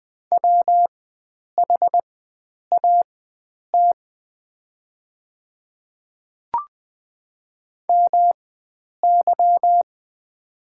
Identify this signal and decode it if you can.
Morse code, heard as a keyed tone, about 20 wpm, spaced as for 9 wpm, 690 Hz. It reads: WHAT MY
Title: 4x Character Spacing - Top 100 Words - 20wpm